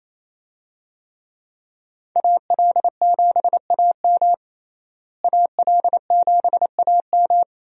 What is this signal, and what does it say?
Morse code, heard as a keyed tone, about 28 wpm, 700 Hz. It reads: AL7AM AL7AM
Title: USA Call Signs - 5 Characters - Repeated 2x - 28wpm